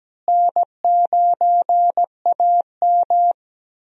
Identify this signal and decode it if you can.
Morse code, heard as a keyed tone, about 17 wpm, 700 Hz. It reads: N9AM
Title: USA Call Signs - 4 Characters - Repeated 2x - 17wpm